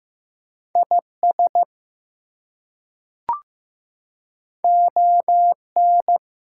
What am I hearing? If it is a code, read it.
Morse code, heard as a keyed tone, about 15 wpm, 700 Hz. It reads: IS ON
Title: Two Letter Words - 15wpm